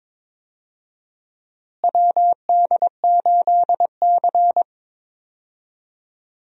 Morse code, heard as a keyed tone, about 22 wpm, 700 Hz. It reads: WD8C
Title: USA Call Signs - 4 Characters - Repeated 2x - 22wpm